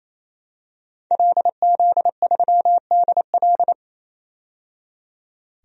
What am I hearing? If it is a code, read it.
Morse code, heard as a keyed tone, about 28 wpm, 700 Hz. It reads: LZ3DL
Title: DXCC-100 Call Signs - 5 Characters - 28wpm